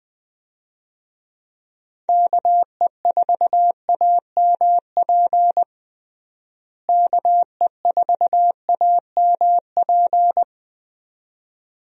Morse code, heard as a keyed tone, about 21 wpm, 700 Hz. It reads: KE4AMP KE4AMP
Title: USA Call Signs - 6 Characters - Repeated 2x - 20wpm